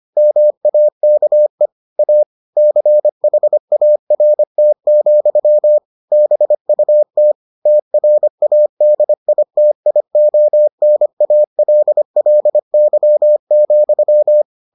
Morse code, heard as a keyed tone, about 25 wpm, 600 Hz. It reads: MAKE A CHART, BUT TRADITIONALLY,